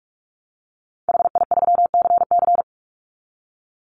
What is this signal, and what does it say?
Morse code, heard as a keyed tone, about 44 wpm, 700 Hz. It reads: HI3CC
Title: Call Signs Repeated 3x - 45wpm